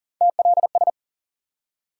Morse code, heard as a keyed tone, about 40 wpm, 700 Hz. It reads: TLS
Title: USA Call Sign Postfixes - 3 Letters 40wpm